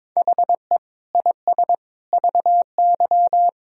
Morse code, heard as a keyed tone, about 21 wpm, 700 Hz. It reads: HE IS VY
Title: Sentences from Top 400 Words and Abbreviations 22wpm